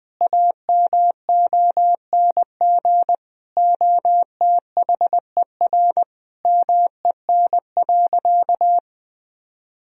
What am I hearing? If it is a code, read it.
Morse code, heard as a keyed tone, about 20 wpm, 700 Hz. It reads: AMONG OTHER MEN.
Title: Sentences from Top 1000 Words - Review Part 3 - 20wpm